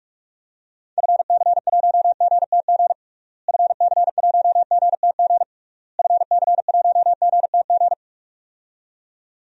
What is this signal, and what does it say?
Morse code, heard as a keyed tone, about 46 wpm, 700 Hz. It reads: FC1GTG FC1GTG FC1GTG
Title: DXCC-100 Call Signs - 6 Characters - Repeated 3x - 45wpm